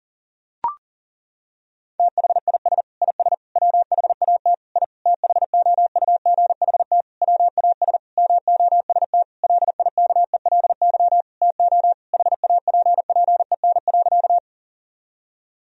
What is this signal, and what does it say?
Morse code, heard as a keyed tone, about 40 wpm, 700 Hz. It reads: THIS IS WHAT I THOUGHT WAS MOST LIKELY TO HAPPEN.
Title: Sentences from Top 1000 Words - Review Part 4 - 40wpm